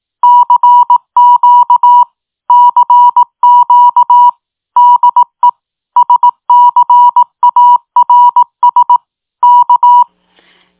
Morse code, heard as a keyed tone, about 18 wpm, 970 Hz. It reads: CQ CQ DE SCARS K